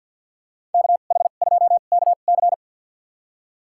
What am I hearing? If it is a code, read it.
Morse code, heard as a keyed tone, about 51 wpm, spaced as for 36 wpm, 700 Hz. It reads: KH2KC
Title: USA Call Signs - 2x Character Spacing - 50wpm